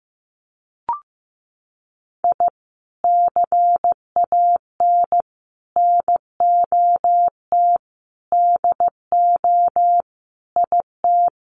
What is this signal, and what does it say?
Morse code, heard as a keyed tone, about 15 wpm, 700 Hz. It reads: I CAN NOT DO IT
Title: Sets of 5 Words Encore 15wpm